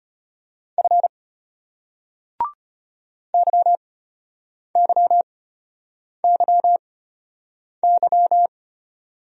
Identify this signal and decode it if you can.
Morse code, heard as a keyed tone, about 34 wpm, spaced as for 8 wpm, 700 Hz. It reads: F YYYY